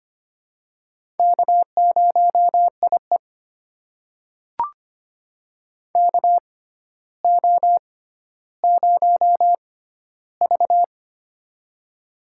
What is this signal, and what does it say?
Morse code, heard as a keyed tone, about 25 wpm, spaced as for 13 wpm, 700 Hz. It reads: K0IE KO0V
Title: USA Call Signs - 4 Characters - 6x Character Spacing - 25wpm